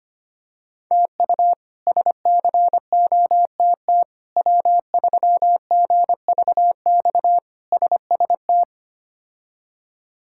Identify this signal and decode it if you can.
Morse code, heard as a keyed tone, about 25 wpm, 700 Hz. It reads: TU SCOTT W3GVX SST
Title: K1USN SST QSO Practice - 25wpm